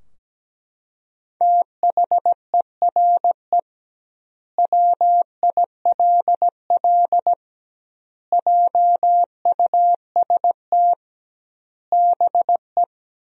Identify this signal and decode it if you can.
Morse code, heard as a keyed tone, about 17 wpm, 700 Hz. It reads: THERE WILL JUST BE
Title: 2x Word-Spacing - Sentences from Top 200 Words - 17wpm